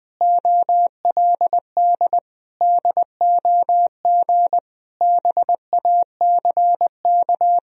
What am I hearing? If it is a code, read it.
Morse code, heard as a keyed tone, about 20 wpm, 700 Hz. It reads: OLD DOG BACK
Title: Sentences from Top 1000 Words - Review Part 2 - 20wpm